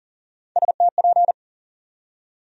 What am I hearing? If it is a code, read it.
Morse code, heard as a keyed tone, about 40 wpm, 700 Hz. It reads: STP
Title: USA Call Sign Postfixes - 3 Letters 40wpm